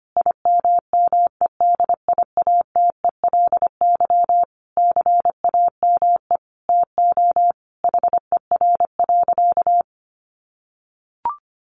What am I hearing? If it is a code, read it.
Morse code, heard as a keyed tone, about 25 wpm, 700 Hz. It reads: IMMEDIATELY CAME TO HER.
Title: Sentences from Top 1100 Words 25wpm